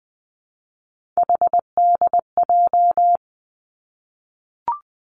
Morse code, heard as a keyed tone, about 20 wpm, 700 Hz. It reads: HDJ
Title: Three Challenging Letter-Number Combinations 20wpm